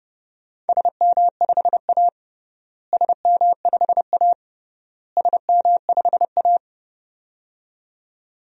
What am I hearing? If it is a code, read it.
Morse code, heard as a keyed tone, about 30 wpm, 700 Hz. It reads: SM5A SM5A SM5A
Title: DXCC-100 Call Signs - 4 Characters - Repeated 3x - 30wpm